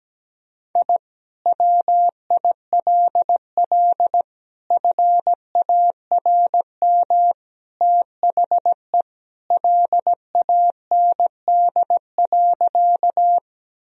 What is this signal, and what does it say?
Morse code, heard as a keyed tone, about 17 wpm, 700 Hz. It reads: I WILL FARM THE LAND.
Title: Sentences from Top 1000 Words - Review Part 2 - 17wpm